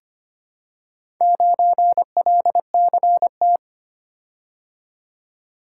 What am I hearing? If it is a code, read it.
Morse code, heard as a keyed tone, about 25 wpm, 700 Hz. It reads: 9LCT
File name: practice / 25wpm / Four Letter-Number Combinations - 25wpm